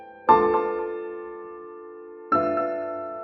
Dial tone (North America), 350 + 440 Hz.